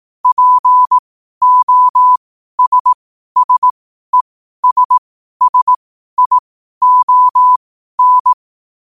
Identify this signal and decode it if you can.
Morse code, heard as a keyed tone, about 18 wpm, spaced as for 12 wpm, 1000 Hz. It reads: POSSESSION
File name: qotdfiles / 12wpm